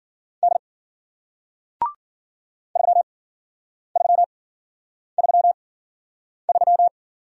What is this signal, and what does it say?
Morse code, heard as a keyed tone, about 52 wpm, spaced as for 9 wpm, 700 Hz. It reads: D 3333